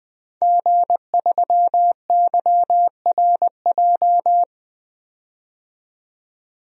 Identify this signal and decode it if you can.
Morse code, heard as a keyed tone, about 20 wpm, 700 Hz. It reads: G3YRJ
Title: DXCC-100 Call Signs - 5 Characters - 20wpm